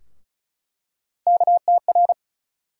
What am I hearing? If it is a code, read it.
Morse code, heard as a keyed tone, about 36 wpm, 700 Hz. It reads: KTR
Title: USA Call Sign Postfixes - 3 Letters 35wpm